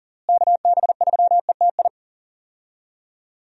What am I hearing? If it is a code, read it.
Morse code, heard as a keyed tone, about 40 wpm, 700 Hz. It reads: KB3ETI
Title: USA Call Signs - 6 Characters - Repeated 3x - 40wpm